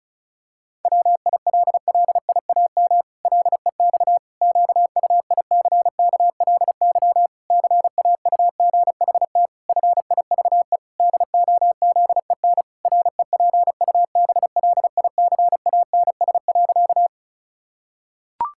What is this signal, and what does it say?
Morse code, heard as a keyed tone, about 34 wpm, 700 Hz. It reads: WILLIAM LEX QUICKLY CAUGHT FIVE DOZEN REPUBLICANS.
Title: Pangram Challenge v2 - 35wpm